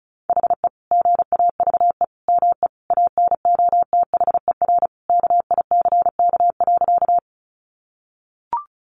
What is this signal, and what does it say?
Morse code, heard as a keyed tone, about 34 wpm, 700 Hz. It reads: HE GAVE ME ANOTHER KICK.
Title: Sentences from Top 1300 Words 35wpm